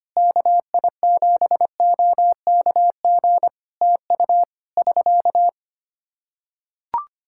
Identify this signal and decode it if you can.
Morse code, heard as a keyed tone, about 25 wpm, 700 Hz. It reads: KI7OKG TU <SK>
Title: K1USN SST QSO Practice 25wpm